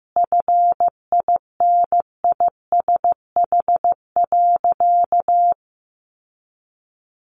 Morse code, heard as a keyed tone, about 15 wpm, 700 Hz. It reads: FINISH.
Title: Sentences from Top 1200 Words 15wpm